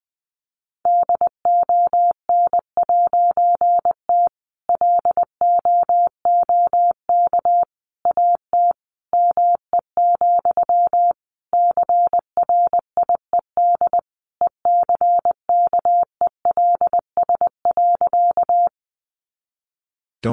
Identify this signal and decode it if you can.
Morse code, heard as a keyed tone, about 20 wpm, 700 Hz. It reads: DON'T LOOK AT ME, CRIED ECKELS.
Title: A Sound Of Thunder 20wpm